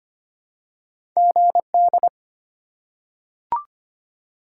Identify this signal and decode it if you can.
Morse code, heard as a keyed tone, about 25 wpm, 700 Hz. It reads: GD
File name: Top 90 QSO Elements - 4x Character Spacing - 25wpm